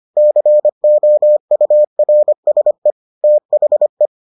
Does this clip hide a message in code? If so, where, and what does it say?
Yes: Morse code from 0.2 to 4.1 s, heard as a keyed tone, about 25 wpm, 600 Hz. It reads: COURSE THE